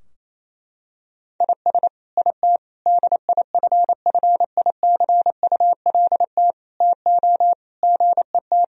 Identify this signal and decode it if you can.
Morse code, heard as a keyed tone, about 28 wpm, 700 Hz. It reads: IS IT DIFFICULT TO GET
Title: Sentences from Top 1000 Words - Review Part 4 - 28wpm